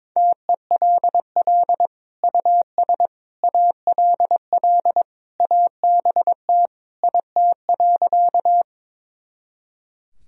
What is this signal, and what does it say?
Morse code, heard as a keyed tone, about 22 wpm, 700 Hz. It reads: TELL US ALL ABT IT.
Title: Sentences from Top 200 Words and Abbreviations 22wpm